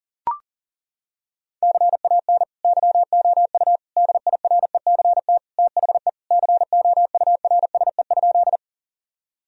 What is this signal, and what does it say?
Morse code, heard as a keyed tone, about 40 wpm, 700 Hz. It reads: CAN YOU DIRECT THE COURSE?
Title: Sentences from Top 1000 Words - Review Part 2 - 40wpm